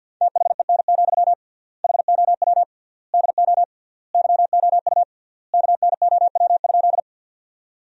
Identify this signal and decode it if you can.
Morse code, heard as a keyed tone, about 50 wpm, 700 Hz. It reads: THEN, HOW DO YOU KNOW?
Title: 3x Word-Spacing - Sentences from Top 100 Words - 50wpm